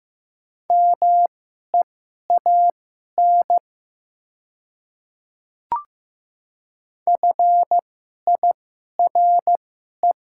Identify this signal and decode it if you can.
Morse code, heard as a keyed tone, about 15 wpm, spaced as for 11 wpm, 700 Hz. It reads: MEAN FIRE
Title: 2x Character Spacing - Top 500 Words - 15wpm